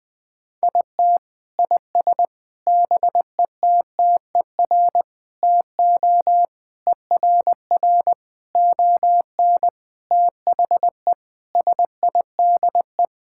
Morse code, heard as a keyed tone, about 20 wpm, 700 Hz. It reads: IT IS BETTER TO ERR ON THE SIDE
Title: Sentences with 7-Word Phrases - 20wpm